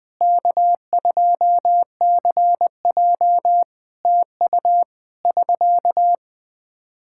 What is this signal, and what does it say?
Morse code, heard as a keyed tone, about 21 wpm, 700 Hz. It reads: K2CJ TU <SK>
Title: K1USN SST QSO Practice 20wpm